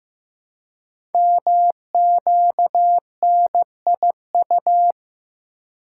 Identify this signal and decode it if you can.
Morse code, heard as a keyed tone, about 15 wpm, 700 Hz. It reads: MQNIU
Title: Code Groups - 5 Letters - 15wpm